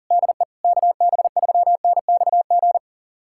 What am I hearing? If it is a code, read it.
Morse code, heard as a keyed tone, about 40 wpm, 700 Hz. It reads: DE KB3NXG
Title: USA Callsign DE Callsign - 6 Characters - 40wpm